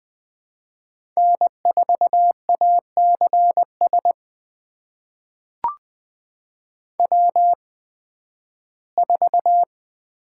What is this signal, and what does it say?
Morse code, heard as a keyed tone, about 20 wpm, spaced as for 13 wpm, 700 Hz. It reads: N4ACS W4
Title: USA Call Signs - 5 Characters - 8x Character Spacing - 20wpm